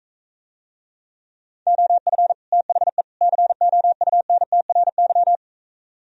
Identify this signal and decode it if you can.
Morse code, heard as a keyed tone, about 45 wpm, 700 Hz. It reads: OF THE COUNTRY